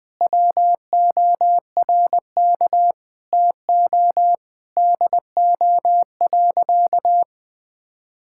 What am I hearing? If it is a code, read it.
Morse code, heard as a keyed tone, about 19 wpm, 700 Hz. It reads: WORK TO DO.
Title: Sentences with 8-Word Phrases - Spoken Beginning - 20wpm